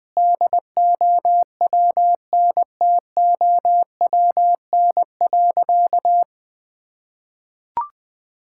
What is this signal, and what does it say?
Morse code, heard as a keyed tone, about 20 wpm, 700 Hz. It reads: DOWNTOWN.
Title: Sentences with 6-Word Phrases - 20wpm